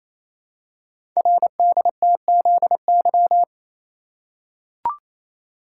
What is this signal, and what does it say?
Morse code, heard as a keyed tone, about 27 wpm, 700 Hz. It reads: RDTZY
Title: Code Groups - 5 Letters - 28wpm